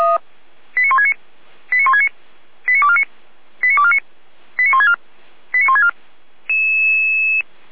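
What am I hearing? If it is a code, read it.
Five-tone selective call (ZVEI), heard as quick runs of five tones, tones of about 80 ms. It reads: 89169 89169 89249 89249 89164 89164
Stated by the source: ZVEI five-tone call received by SDR.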